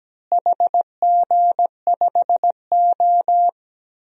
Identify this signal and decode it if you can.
Morse code, heard as a keyed tone, about 17 wpm, 700 Hz. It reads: HG5O
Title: DXCC-100 Call Signs - 4 Characters - Repeated 2x - 17wpm